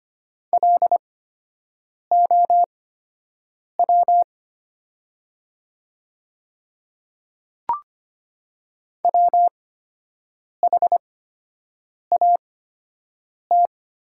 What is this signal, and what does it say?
Morse code, heard as a keyed tone, about 25 wpm, spaced as for 7 wpm, 700 Hz. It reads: LOW WHAT
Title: 8x Character Spacing - Top 200 Words - 25wpm